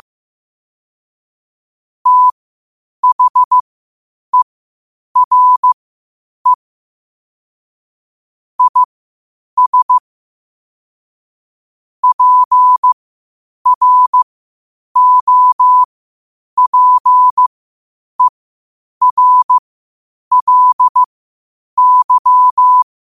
Morse code, heard as a keyed tone, about 15 wpm, spaced as for 8 wpm, 1000 Hz. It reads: THERE IS PROPERLY
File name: qotdfiles / 8wpm